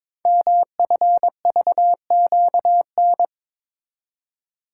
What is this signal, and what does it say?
Morse code, heard as a keyed tone, about 22 wpm, 700 Hz. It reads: MFVQN
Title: Code Groups - 5 Letters - 22wpm